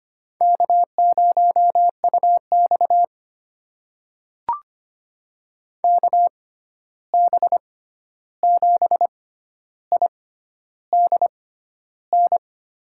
Morse code, heard as a keyed tone, about 25 wpm, spaced as for 11 wpm, 700 Hz. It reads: K0UX KB7IDN
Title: USA Call Signs - 6x Character Spacing - 25wpm